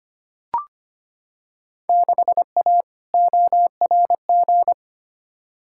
Morse code, heard as a keyed tone, about 25 wpm, 700 Hz. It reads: 6A ORG
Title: Field Day Exchanges Encore - 25wpm